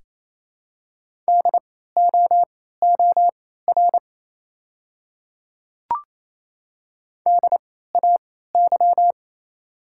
Morse code, heard as a keyed tone, about 28 wpm, spaced as for 16 wpm, 700 Hz. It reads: DOOR DAY
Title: 3x Character Spacing - Top 300 Words - 28wpm